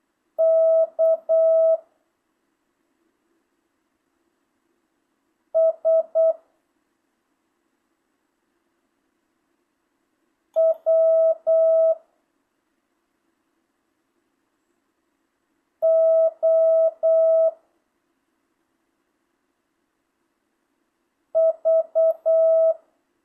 Morse code, heard as a keyed tone, about 8 wpm, 650 Hz. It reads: K S W O V
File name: jh8cbh / 2025cw